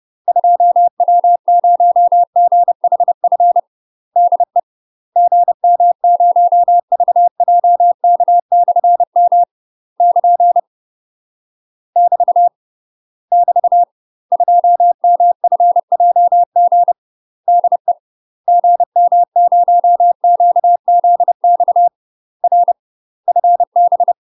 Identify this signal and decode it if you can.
Morse code, heard as a keyed tone, about 30 wpm, 700 Hz. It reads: 2W0GHF DE GM0VJK/M <KN> <BT> <BT> 2MFJG DE GM0QZX R FB
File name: UK MASTER QSOs TRAINING / 30WPM / UK MASTER QSOs 0013